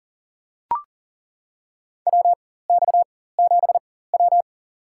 Morse code, heard as a keyed tone, about 40 wpm, spaced as for 19 wpm, 700 Hz. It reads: WX7W
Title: USA Call Signs - 4 Characters - 4x Character Spacing - 40wpm